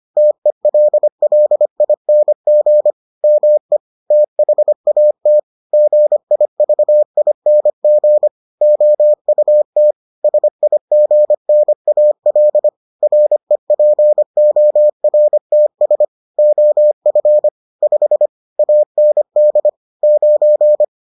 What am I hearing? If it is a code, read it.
Morse code, heard as a keyed tone, about 25 wpm, 600 Hz. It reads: TELLING ME THAT GIVING OUT SIGNAL REPORTS OF 5 AND 9